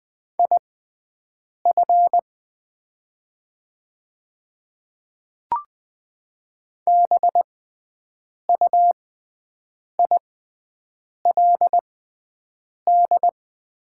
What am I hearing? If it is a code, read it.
Morse code, heard as a keyed tone, about 20 wpm, spaced as for 7 wpm, 700 Hz. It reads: IF BUILD
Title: 6x Character Spacing - Top 200 Words - 20wpm